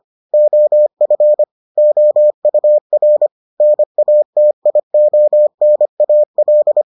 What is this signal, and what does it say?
Morse code, heard as a keyed tone, about 25 wpm, 600 Hz. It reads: OF OUR NATIONAL